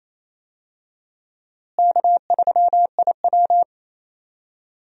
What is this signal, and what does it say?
Morse code, heard as a keyed tone, about 28 wpm, 700 Hz. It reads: K3IW
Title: USA Call Signs - 4 Characters - Repeated 3x - 28wpm